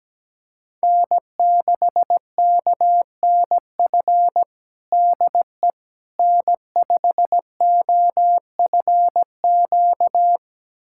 Morse code, heard as a keyed tone, about 17 wpm, 700 Hz. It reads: N6KNF DE N5OFQ
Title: USA Callsign DE Callsign - 5 Characters - 17wpm